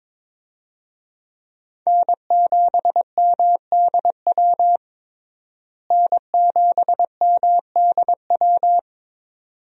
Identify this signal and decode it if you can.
Morse code, heard as a keyed tone, about 22 wpm, 700 Hz. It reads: N7MDW N7MDW
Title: USA Call Signs - 5 Characters - Repeated 2x - 22wpm